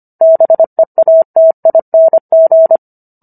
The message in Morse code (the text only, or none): BEATING